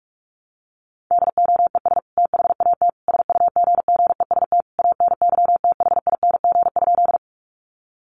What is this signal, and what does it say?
Morse code, heard as a keyed tone, about 44 wpm, 700 Hz. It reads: DOES THAT SUGGEST ANYTHING?
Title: Sentences from Top 1300 Words 45wpm